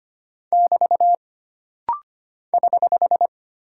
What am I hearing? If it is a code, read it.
Morse code, heard as a keyed tone, about 25 wpm, 700 Hz. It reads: <BT> <HH>
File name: Single Punctuation-Prosign - Rapid-Fire - 25wpm